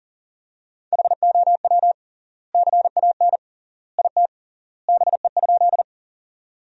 Morse code, heard as a keyed tone, about 40 wpm, 700 Hz. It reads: HOW CAN IT BE?
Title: 3x Word-Spacing - Sentences from Top 100 Words - 40wpm